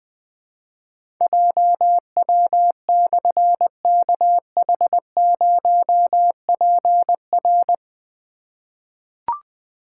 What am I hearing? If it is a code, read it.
Morse code, heard as a keyed tone, about 21 wpm, 700 Hz. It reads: JW/KH0PR
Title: Difficult Call Signs 20wpm